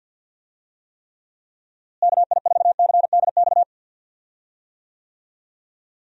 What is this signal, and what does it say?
Morse code, heard as a keyed tone, about 48 wpm, 700 Hz. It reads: KI4CDX
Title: USA Call Signs - 6 Characters - Repeated 2x - 50wpm